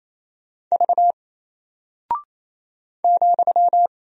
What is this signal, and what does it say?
Morse code, heard as a keyed tone, about 27 wpm, 700 Hz. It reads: V ,
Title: Single Character (Letter:Number:Punctuation:Prosign) 28wpm